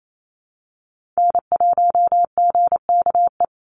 Morse code, heard as a keyed tone, about 28 wpm, 700 Hz. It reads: N1GKE
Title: Call Signs Repeated 3x - 28wpm